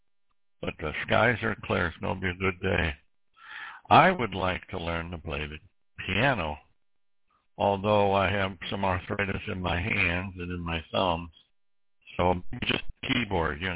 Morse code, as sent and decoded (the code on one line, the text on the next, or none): none